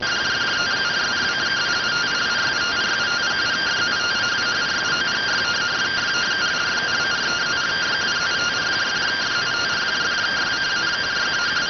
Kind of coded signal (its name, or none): two-tone teleprinter signal